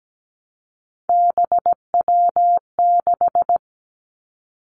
Morse code, heard as a keyed tone, about 17 wpm, 700 Hz. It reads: BW6
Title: Three Challenging Letter-Number Combinations 17wpm